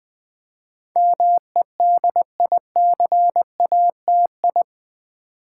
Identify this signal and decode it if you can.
Morse code, heard as a keyed tone, about 20 wpm, 700 Hz. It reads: MEDICATI
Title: Build A Word - 4 Letters At A Time - 10-Letter Words - 20wpm